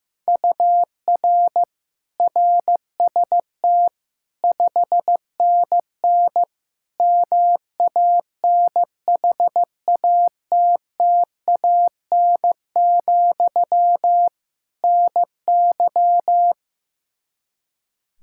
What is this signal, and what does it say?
Morse code, heard as a keyed tone, about 16 wpm, 700 Hz. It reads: UR RST 5NN MANHATTAN, NY
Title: Practice QSO - Exchange 1 - Beginning - 15wpm